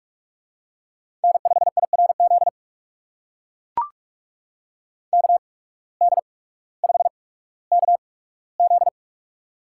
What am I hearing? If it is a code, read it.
Morse code, heard as a keyed tone, about 44 wpm, spaced as for 19 wpm, 700 Hz. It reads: N5IRZ KD5KZ
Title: USA Call Signs - 5 Characters - 8x Character Spacing - 45wpm